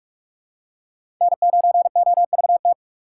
Morse code, heard as a keyed tone, about 45 wpm, 700 Hz. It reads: N9OVT
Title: USA Call Signs - 5 Characters - 6x Character Spacing - 45wpm